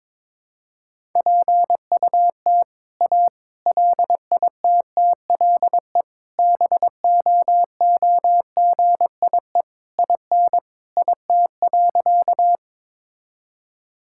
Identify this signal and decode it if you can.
Morse code, heard as a keyed tone, about 22 wpm, 700 Hz. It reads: PUT A LITTLE BOOGIE IN IT.
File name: All-Age Appropriate Jokes - 22wpm